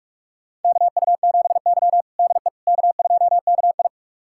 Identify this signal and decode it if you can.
Morse code, heard as a keyed tone, about 47 wpm, 700 Hz. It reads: KU7Y DE K2KI
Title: USA Callsign DE Callsign - 4 Characters - 45wpm